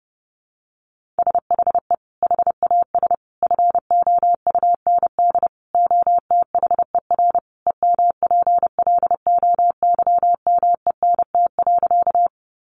Morse code, heard as a keyed tone, about 30 wpm, 700 Hz. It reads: SHE HAS FOUND OTHER EMPLOYMENT.
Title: Sentences from Top 1500 Words 30wpm